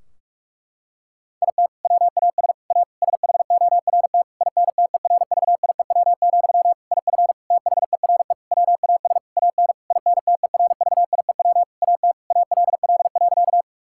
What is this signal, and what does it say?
Morse code, heard as a keyed tone, about 46 wpm, 700 Hz. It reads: IT WAS A SHORT INTERVIEW, IF THERE WAS AN INTERVIEW AT ALL.
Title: Sentences from Top 1600 Words 45wpm